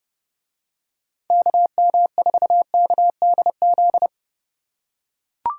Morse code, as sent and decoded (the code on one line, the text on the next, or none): -.- -- ....- -.- -.. --..
KM4KDZ